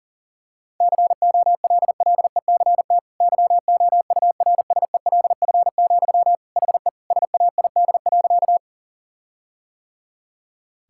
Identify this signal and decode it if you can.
Morse code, heard as a keyed tone, about 40 wpm, 700 Hz. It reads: COLLECT YOURSELF, HE SAID.
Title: Sentences from Top 1600 Words 40wpm